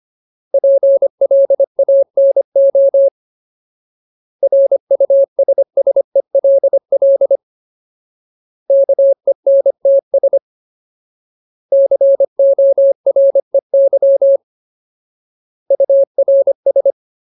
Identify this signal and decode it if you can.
Morse code, heard as a keyed tone, about 25 wpm, 550 Hz. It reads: PLANO RUSSELL KENTS COREY URS
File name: Ultimate Word List / 25WPM Ultimate